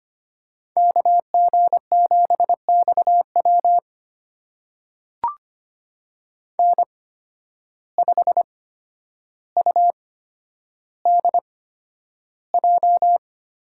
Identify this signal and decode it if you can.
Morse code, heard as a keyed tone, about 25 wpm, spaced as for 11 wpm, 700 Hz. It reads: KG7XW N5UDJ
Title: USA Call Signs - 5 Characters - 8x Character Spacing - 25wpm